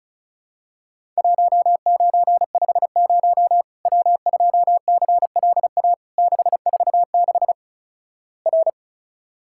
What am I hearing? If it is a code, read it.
Morse code, heard as a keyed tone, about 36 wpm, 700 Hz. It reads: 1950 W2CLA 646 R
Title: Worked All Europe DX Contest - QTC Exchange - 35wpm